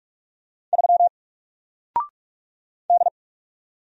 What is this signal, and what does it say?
Morse code, heard as a keyed tone, about 44 wpm, 700 Hz. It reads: 3 D
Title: Single Letter-Number 45wpm